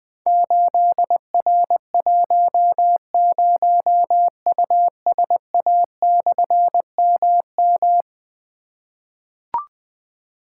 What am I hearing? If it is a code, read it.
Morse code, heard as a keyed tone, about 20 wpm, 700 Hz. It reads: 8R10USA/MM
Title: Difficult Call Signs 20wpm